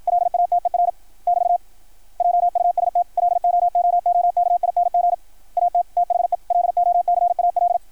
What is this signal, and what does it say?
Morse code, heard as a keyed tone, about 53 wpm, 660 Hz. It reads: LATER <BT> JUST LOOOKING AT THE LOCAL
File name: archived / CRSnet